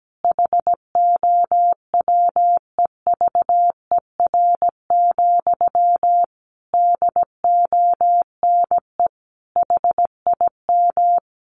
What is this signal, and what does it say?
Morse code, heard as a keyed tone, about 17 wpm, 700 Hz. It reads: HOWEVER, DONE HIM
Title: Sentences from Top 1200 Words 17wpm